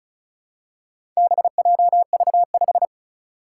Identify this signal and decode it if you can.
Morse code, heard as a keyed tone, about 36 wpm, 700 Hz. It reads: BJV5
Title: Four Challenging Letter-Number Combinations 35wpm